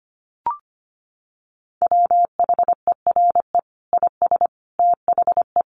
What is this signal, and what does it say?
Morse code, heard as a keyed tone, about 25 wpm, 700 Hz. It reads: WHERE IS THE
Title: Sentences from Top 1400 Words 25wpm